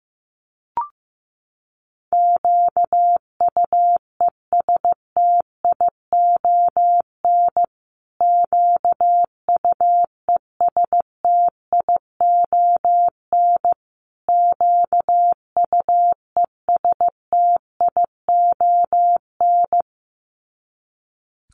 Morse code, heard as a keyed tone, about 15 wpm, 700 Hz. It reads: QUESTION QUESTION QUESTION
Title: Top 500 Words Repeated 15wpm